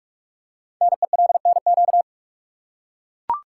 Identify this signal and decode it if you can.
Morse code, heard as a keyed tone, about 44 wpm, 690 Hz. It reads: NELNQ
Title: Code Groups - 5 Letters - 45wpm